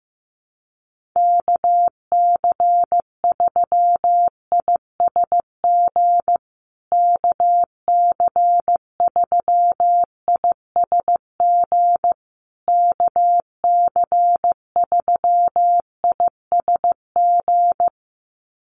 Morse code, heard as a keyed tone, about 14 wpm, 700 Hz. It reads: KC3ISG KC3ISG KC3ISG
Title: USA Call Signs Repeated 3x 15wpm